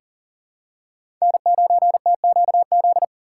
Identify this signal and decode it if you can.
Morse code, heard as a keyed tone, about 40 wpm, 700 Hz. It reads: N9TQZ
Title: USA Call Signs - 5 Characters - 6x Character Spacing - 40wpm